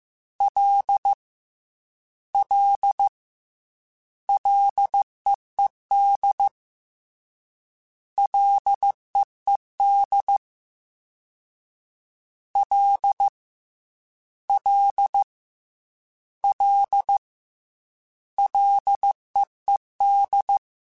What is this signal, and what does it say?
Morse code, heard as a keyed tone, about 15 wpm, 800 Hz. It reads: L L LEED LEED L L L LEED
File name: Ls / Single L m3s / Single L Prefixe 3x2 Twice @13